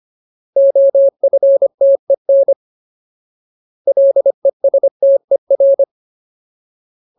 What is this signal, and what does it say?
Morse code, heard as a keyed tone, about 25 wpm, 550 Hz. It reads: OFTEN LESTER
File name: Ultimate Word List / 25WPM Ultimate